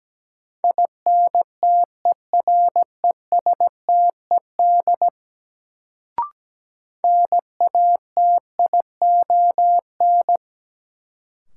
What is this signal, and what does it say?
Morse code, heard as a keyed tone, about 17 wpm, 700 Hz. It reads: INTERESTED NATION
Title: Sentences from Top 900 Words - New Word Drill - 17wpm